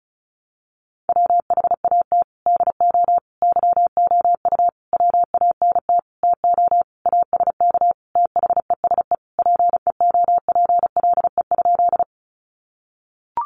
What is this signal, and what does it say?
Morse code, heard as a keyed tone, about 34 wpm, 700 Hz. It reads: WHAT DO YOU WANT TO ASK THESE PEOPLE?
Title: Sentences from Top 200 Words 35wpm